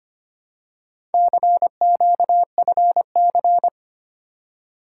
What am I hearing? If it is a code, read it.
Morse code, heard as a keyed tone, about 25 wpm, 700 Hz. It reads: CQFC
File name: practice / 25wpm / Four Letter-Number Combinations - 25wpm